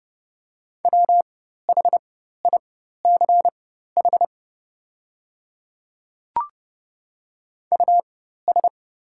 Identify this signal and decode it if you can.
Morse code, heard as a keyed tone, about 30 wpm, spaced as for 14 wpm, 700 Hz. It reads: WHICH US
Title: 4x Character Spacing - Top 200 Words - 30wpm